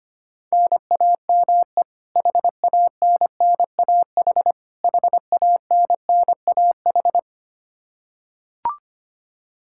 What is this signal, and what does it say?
Morse code, heard as a keyed tone, about 25 wpm, 700 Hz. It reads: NAME HANNAH HANNAH
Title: Practice QSO - Exchange 1 - Beginning - 25wpm